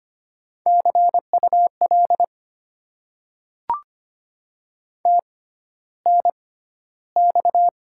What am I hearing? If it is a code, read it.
Morse code, heard as a keyed tone, about 25 wpm, spaced as for 13 wpm, 690 Hz. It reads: CUL TNX
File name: Top 90 QSO Elements - 6x Character Spacing - 25wpm